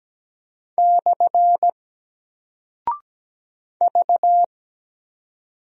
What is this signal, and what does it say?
Morse code, heard as a keyed tone, about 17 wpm, 700 Hz. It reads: / V
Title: Single Character (Letter:Number:Punctuation:Prosign) Alt 17wpm